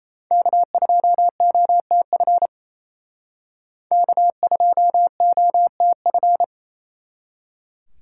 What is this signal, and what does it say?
Morse code, heard as a keyed tone, about 36 wpm, 700 Hz. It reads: K2OTF K2OTF